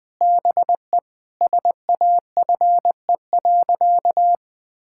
Morse code, heard as a keyed tone, about 19 wpm, 700 Hz. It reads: BE SAFE.
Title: Sentences from Top 1600 Words 20wpm